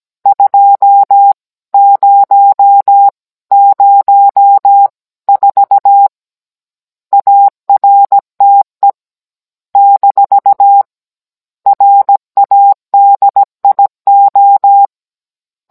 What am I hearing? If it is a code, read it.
Morse code, heard as a keyed tone, about 17 wpm, spaced as for 13 wpm, 800 Hz. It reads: 2004 ARTE-RADIO